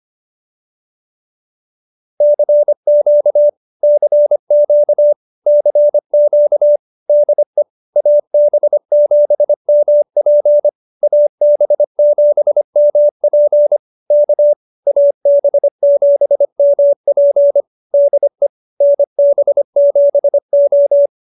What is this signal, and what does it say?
Morse code, heard as a keyed tone, about 25 wpm, 590 Hz. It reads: CQ CQ CQ DE AB7MP AB7MP K AB7MP DE NB7O
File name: QSO306 25